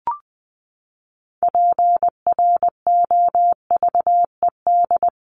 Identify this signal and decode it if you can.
Morse code, heard as a keyed tone, about 20 wpm, 700 Hz. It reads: PROVED
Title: ED Suffix - Top 100 Words with Base Words 20wpm